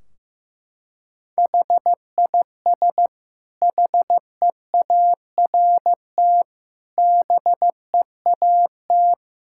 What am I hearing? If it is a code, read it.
Morse code, heard as a keyed tone, about 15 wpm, 700 Hz. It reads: HIS HEART BEAT
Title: Sentences from Top 1600 Words 15wpm